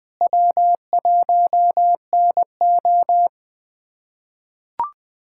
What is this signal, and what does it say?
Morse code, heard as a keyed tone, about 20 wpm, 700 Hz. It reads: W1NO
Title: USA Callsign DE Callsign - 4 Characters - 20wpm